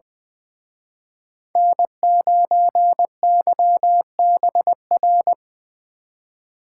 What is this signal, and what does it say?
Morse code, heard as a keyed tone, about 24 wpm, 700 Hz. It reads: N9YBR
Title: Speed Racing - USA 1x3 Call Signs 20wpm